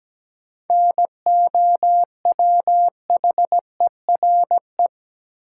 Morse code, heard as a keyed tone, about 17 wpm, 700 Hz. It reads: NOWHERE
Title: Sets of 5 Words Encore - New Word Drill - 17wpm